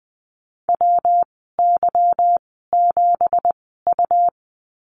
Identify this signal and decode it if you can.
Morse code, heard as a keyed tone, about 20 wpm, spaced as for 15 wpm, 700 Hz. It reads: WY7U
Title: 231 Lesson 37g - Letter x - Call Signs 20wpm